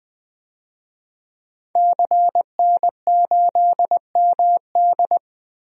Morse code, heard as a keyed tone, about 21 wpm, 700 Hz. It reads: CN8MD
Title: DXCC-100 Call Signs - 5 Characters - Repeated 2x - 20wpm